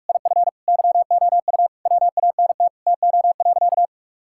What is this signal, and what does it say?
Morse code, heard as a keyed tone, about 46 wpm, 700 Hz. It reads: IF YOU WANT TO.
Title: Sentences from Top 1600 Words 45wpm